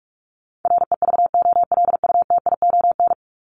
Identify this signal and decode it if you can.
Morse code, heard as a keyed tone, about 44 wpm, 700 Hz. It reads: REVOLUTION
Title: Top 1751-2000 Words 45wpm